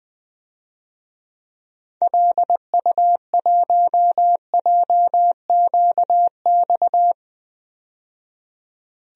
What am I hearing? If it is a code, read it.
Morse code, heard as a keyed tone, about 20 wpm, 700 Hz. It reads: LU1JQX
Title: DXCC-100 Call Signs - 6 Characters - 20wpm